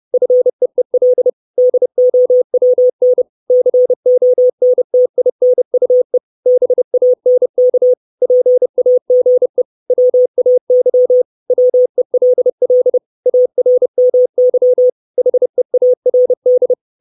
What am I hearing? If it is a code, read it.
Morse code, heard as a keyed tone, about 30 wpm, 500 Hz. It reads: FEEL DOWN CONTINUE BANK PAGE WAY WELL ARMY HEARD
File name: 632 Words 30WPM